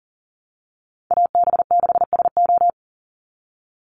Morse code, heard as a keyed tone, about 40 wpm, 700 Hz. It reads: AB6SO
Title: NA SOTA Chaser Callsigns 40wpm